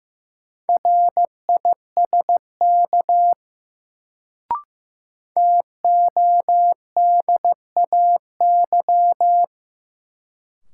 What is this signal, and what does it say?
Morse code, heard as a keyed tone, about 15 wpm, 700 Hz. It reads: RISK TODAY
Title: Sentences from Top 600 Words - New Word Drill - 15wpm